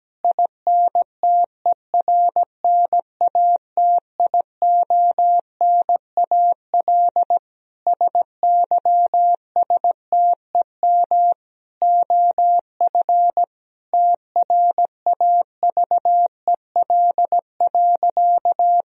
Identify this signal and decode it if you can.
Morse code, heard as a keyed tone, about 17 wpm, 700 Hz. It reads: INTERNATIONAL SYSTEM OF TRAVEL.
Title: Sentences from Top 1000 Words - Review Part 4 - 17wpm